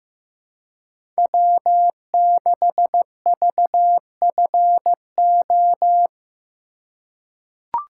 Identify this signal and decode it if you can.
Morse code, heard as a keyed tone, about 15 wpm, 700 Hz. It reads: W6VFO
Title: USA Call Signs - 5 Characters - 2x Character Spacing - 15wpm